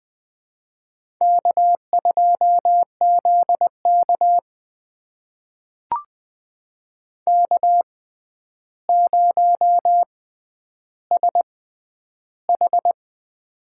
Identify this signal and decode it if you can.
Morse code, heard as a keyed tone, about 20 wpm, spaced as for 10 wpm, 700 Hz. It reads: K2ZK K0SH
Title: USA Call Signs - 4 Characters - 6x Character Spacing - 20wpm